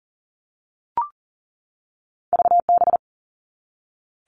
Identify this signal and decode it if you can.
Morse code, heard as a keyed tone, about 40 wpm, 700 Hz. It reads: VB
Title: All Cut Numbers - 2-Digit 40wpm